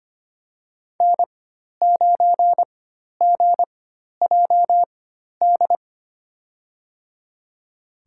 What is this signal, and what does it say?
Morse code, heard as a keyed tone, about 25 wpm, spaced as for 12 wpm, 700 Hz. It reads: N9GJD
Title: USA Call Signs - 5 Characters - 4x Character Spacing - 25wpm